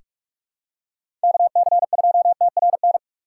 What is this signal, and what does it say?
Morse code, heard as a keyed tone, about 41 wpm, 700 Hz. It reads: KC2TRN
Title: USA Call Signs - 6 Characters - 3x Character Spacing - 45wpm